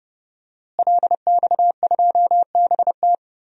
Morse code, heard as a keyed tone, about 31 wpm, 700 Hz. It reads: LX2BT
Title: DXCC-100 Call Signs - 5 Characters - Repeated 2x - 30wpm